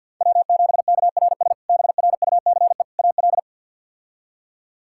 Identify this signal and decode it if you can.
Morse code, heard as a keyed tone, about 52 wpm, 700 Hz. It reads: W7KRS BRUCE AL
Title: K1USN SST QSO Practice - 50wpm